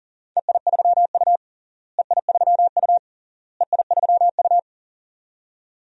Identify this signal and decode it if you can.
Morse code, heard as a keyed tone, about 40 wpm, 700 Hz. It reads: EI3U EI3U EI3U
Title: DXCC-100 Call Signs - 4 Characters - Repeated 3x - 40wpm